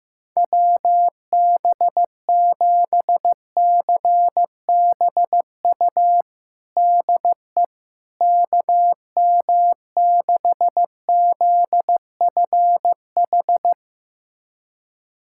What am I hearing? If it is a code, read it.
Morse code, heard as a keyed tone, about 15 wpm, 700 Hz. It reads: WB7CBU DE KM6ZFH
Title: USA Callsign DE Callsign - 6 Characters - 15wpm